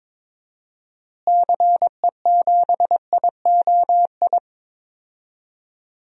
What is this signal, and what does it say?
Morse code, heard as a keyed tone, about 22 wpm, 700 Hz. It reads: CE7IOI
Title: DXCC-100 Call Signs - 6 Characters - Repeated 3x - 22wpm